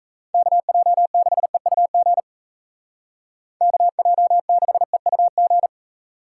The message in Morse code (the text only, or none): KJ6EUG KJ6EUG